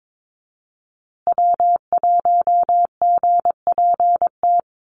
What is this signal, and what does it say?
Morse code, heard as a keyed tone, about 22 wpm, 700 Hz. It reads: W1GPT
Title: USA Call Signs Repeated 3x 22wpm